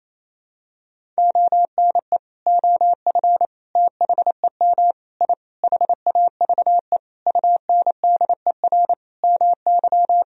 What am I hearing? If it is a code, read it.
Morse code, heard as a keyed tone, about 28 wpm, 700 Hz. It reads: ONE OF THEM I HAVE UNDER MY
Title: Sentences from Top 1000 Words - Review Part 1 - 28wpm